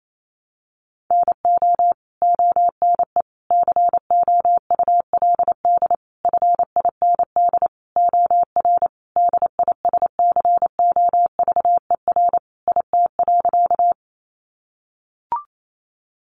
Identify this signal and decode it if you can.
Morse code, heard as a keyed tone, about 28 wpm, 700 Hz. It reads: NO ONE COULD FIND OR DISCOVER IT.
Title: Sentences from Top 1100 Words 28wpm